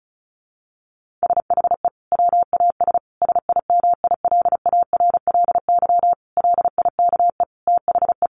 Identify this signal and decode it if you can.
Morse code, heard as a keyed tone, about 34 wpm, 700 Hz. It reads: SHE WAS SIMILARLY LIKE THE
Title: Sentences from Top 1400 Words 35wpm